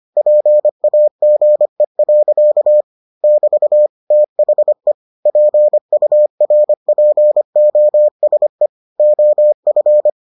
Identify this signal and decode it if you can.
Morse code, heard as a keyed tone, about 25 wpm, 600 Hz. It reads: PAGE. <BT> THE PURPOSE OF